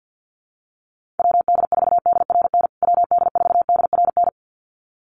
Morse code, heard as a keyed tone, about 50 wpm, 700 Hz. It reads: WD4DRN WD4DRN
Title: USA Call Signs Repeated 2x 50wpm